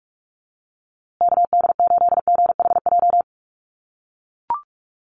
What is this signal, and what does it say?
Morse code, heard as a keyed tone, about 44 wpm, 700 Hz. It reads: KD8GHJ
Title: USA Call Signs Repeated 2x 45wpm